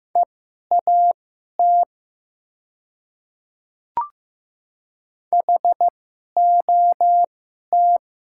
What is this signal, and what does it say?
Morse code, heard as a keyed tone, about 15 wpm, spaced as for 11 wpm, 700 Hz. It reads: EAT HOT
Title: 2x Character Spacing - Top 500 Words - 15wpm